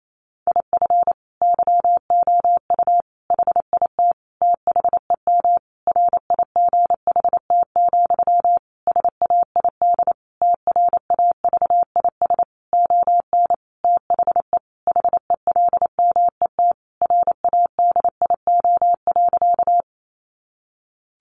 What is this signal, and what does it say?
Morse code, heard as a keyed tone, about 28 wpm, 700 Hz. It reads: IF YOU HIT THEM RIGHT, SAID TRAVIS ON THE HELMET RADIO.
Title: A Sound Of Thunder 28wpm